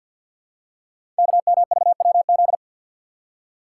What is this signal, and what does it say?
Morse code, heard as a keyed tone, about 52 wpm, 700 Hz. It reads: KMVWZ
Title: Code Groups - 5 Letters - 50wpm